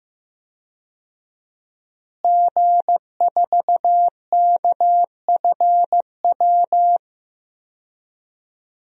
Morse code, heard as a keyed tone, about 15 wpm, 700 Hz. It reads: G4KFW
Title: DXCC-100 Call Signs - 5 Characters - Repeated 2x - 15wpm